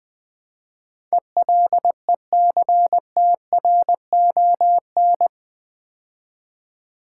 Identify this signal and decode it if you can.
Morse code, heard as a keyed tone, about 20 wpm, 700 Hz. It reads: ELECTRON
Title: Build A Word - 4 Letters At A Time - 11-Letter Words - 20wpm